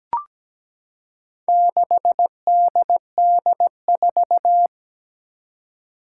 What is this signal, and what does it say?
Morse code, heard as a keyed tone, about 17 wpm, 700 Hz. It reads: 6DD4
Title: Four Challenging Letter-Number Combinations 17wpm